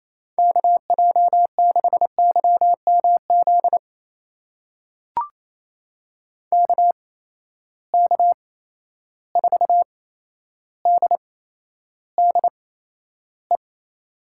Morse code, heard as a keyed tone, about 28 wpm, spaced as for 12 wpm, 700 Hz. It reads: KJ6YMZ KK4DDE
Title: USA Call Signs - 6 Characters - 8x Character Spacing - 28wpm